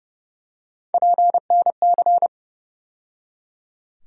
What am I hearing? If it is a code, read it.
Morse code, heard as a keyed tone, about 30 wpm, 700 Hz. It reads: PNC
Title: USA Call Sign Postfixes - 3 Letters 30wpm